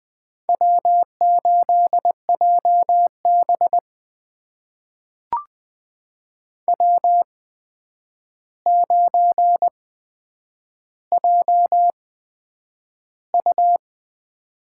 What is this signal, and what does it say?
Morse code, heard as a keyed tone, about 20 wpm, spaced as for 9 wpm, 700 Hz. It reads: W8JB W9JU
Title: USA Call Signs - 4 Characters - 8x Character Spacing - 20wpm